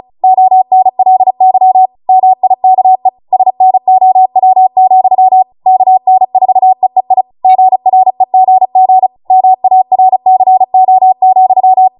Morse code, heard as a keyed tone, about 36 wpm, 750 Hz. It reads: ONLY MIKE SNOW, KN4EEI GREGG MARCO,